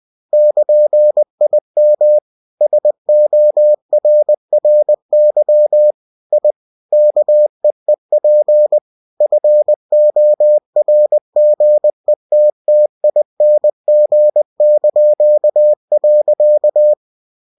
Morse code, heard as a keyed tone, about 20 wpm, 600 Hz. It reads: <KN>IM SORRY I KEEP FORGETTING).
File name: cw-books / winnie-the-pooh / winnie-20wpm-page-0104